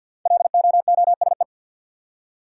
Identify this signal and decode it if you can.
Morse code, heard as a keyed tone, about 46 wpm, 700 Hz. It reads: LOOSE
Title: Five Letter Words - 50wpm